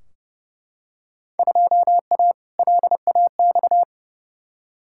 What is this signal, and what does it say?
Morse code, heard as a keyed tone, about 32 wpm, 700 Hz. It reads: 2A LAX
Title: Field Day Exchanges - 30wpm